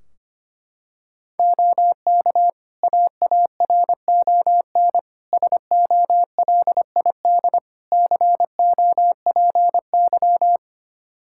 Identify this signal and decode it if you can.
Morse code, heard as a keyed tone, about 25 wpm, 700 Hz. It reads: OK AARON SOLID COPY
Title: K1USN SST QSO Practice 25wpm